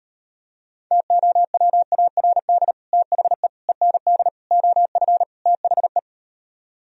Morse code, heard as a keyed tone, about 38 wpm, 700 Hz. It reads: TOWARD THE END OF THE